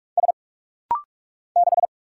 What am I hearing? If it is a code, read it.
Morse code, heard as a keyed tone, about 46 wpm, 700 Hz. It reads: S 6
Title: Single Letter-Number - Rapid-Fire 45wpm